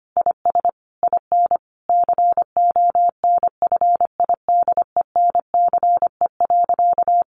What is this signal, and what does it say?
Morse code, heard as a keyed tone, about 25 wpm, 700 Hz. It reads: IS IN CONFIDENCE.
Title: Sentences from Top 1500 Words 25wpm